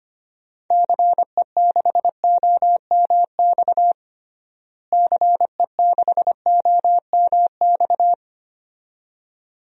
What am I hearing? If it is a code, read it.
Morse code, heard as a keyed tone, about 25 wpm, 700 Hz. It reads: CE6OMX CE6OMX
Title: DXCC-100 Call Signs - 6 Characters - Repeated 2x - 25wpm